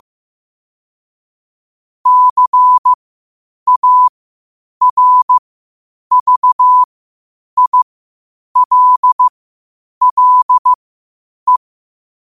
Morse code, heard as a keyed tone, about 15 wpm, spaced as for 8 wpm, 1000 Hz. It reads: CARVILLE
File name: qotdfiles / 8wpm